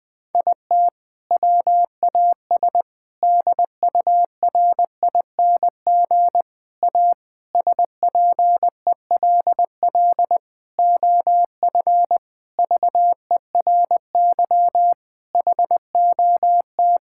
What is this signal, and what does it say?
Morse code, heard as a keyed tone, about 20 wpm, 700 Hz. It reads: IT WAS DURING A SPELL OF VERY HOT
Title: Sentences from Top 1000 Words - Review Part 5 - 20wpm